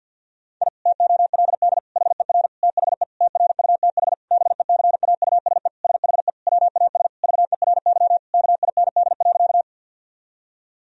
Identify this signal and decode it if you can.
Morse code, heard as a keyed tone, about 52 wpm, 700 Hz. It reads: I TOLD HER THE TRUTH BECAUSE SHE WAS VERY KIND.
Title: Sentences from Top 1000 Words - Review Part 3 - 50wpm